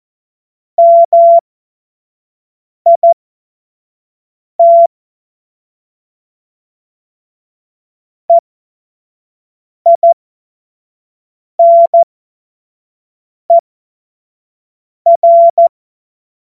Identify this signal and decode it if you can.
Morse code, heard as a keyed tone, about 14 wpm, spaced as for 5 wpm, 680 Hz. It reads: MIT EINER
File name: Morsekurs / Witze / Witz 05 Dezember 2023 25 F70